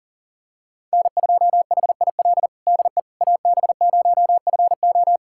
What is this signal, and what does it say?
Morse code, heard as a keyed tone, about 40 wpm, 700 Hz. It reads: N2HIL DE AB0FO
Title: USA Callsign DE Callsign - 5 Characters - 40wpm